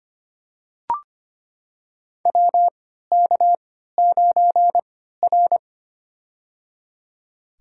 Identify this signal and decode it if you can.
Morse code, heard as a keyed tone, about 25 wpm, spaced as for 14 wpm, 700 Hz. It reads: WK9R
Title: USA Call Signs - 3x Character Spacing - 25wpm